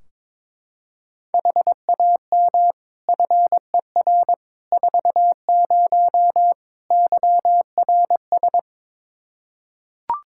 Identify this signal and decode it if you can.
Morse code, heard as a keyed tone, about 21 wpm, 700 Hz. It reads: HAM FER 40 YRS
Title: Practice QSO - Exchange 3 - More Info - 22wpm